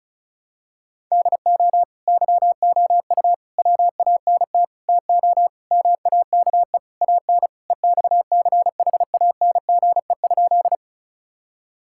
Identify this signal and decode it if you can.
Morse code, heard as a keyed tone, about 34 wpm, 700 Hz. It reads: DO YOU WANT TO MAKE AN EXCHANGE?
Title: Sentences from Top 1000 Words - Review Part 6 - 35wpm